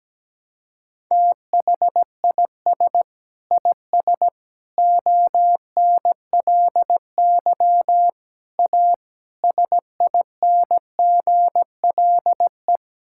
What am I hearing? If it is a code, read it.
Morse code, heard as a keyed tone, about 17 wpm, 700 Hz. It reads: THIS IS ONLY A SINGLE
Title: Sentences from Top 1000 Words - Review Part 5 - 17wpm